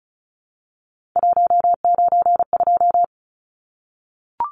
Morse code, heard as a keyed tone, about 34 wpm, 700 Hz. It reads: 192
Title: Numbers - 3-Digit 35wpm